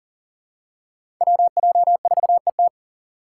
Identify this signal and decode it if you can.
Morse code, heard as a keyed tone, about 40 wpm, 700 Hz. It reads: WJ4ET
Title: USA Call Signs - 2x Character Spacing - 40wpm